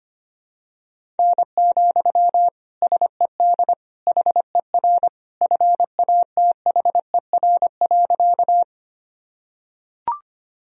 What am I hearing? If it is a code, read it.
Morse code, heard as a keyed tone, about 25 wpm, 700 Hz. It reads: N, SED HER FATHER.
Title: Sentences from Top 300 Words and Abbreviations 25wpm